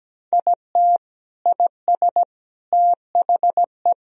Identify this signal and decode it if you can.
Morse code, heard as a keyed tone, about 17 wpm, 700 Hz. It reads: IT IS THE
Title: Sentences from Top 1000 Words - Review Part 2 - 17wpm